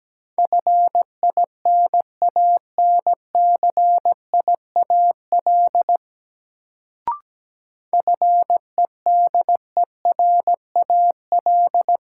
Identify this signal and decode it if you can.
Morse code, heard as a keyed tone, about 17 wpm, 700 Hz. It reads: FINANCIAL FEDERAL
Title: Sentences from Top 1000 Words - Review Part 5 - New Word Drill - 17wpm